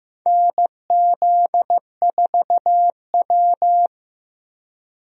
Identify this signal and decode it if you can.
Morse code, heard as a keyed tone, about 15 wpm, 700 Hz. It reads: NZ4W
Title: USA Call Signs - 4 Characters - Repeated 3x - 15wpm